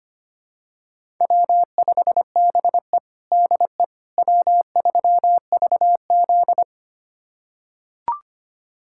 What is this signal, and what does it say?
Morse code, heard as a keyed tone, about 25 wpm, 700 Hz. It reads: W5BE DE W3VZ
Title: USA Callsign DE Callsign - 4 Characters - 25wpm